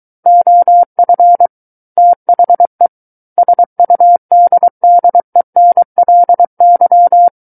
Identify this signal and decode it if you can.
Morse code, heard as a keyed tone, about 23 wpm, 700 Hz. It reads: OF THE SUDDENLY
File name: Book One / 14-Chapter 14 - 23 WPM